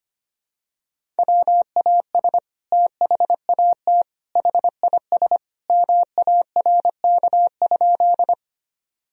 Morse code, heard as a keyed tone, about 25 wpm, 700 Hz. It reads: WAS THAT HIS MARK?
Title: Sentences from Top 1000 Words - Review Part 1 - 25wpm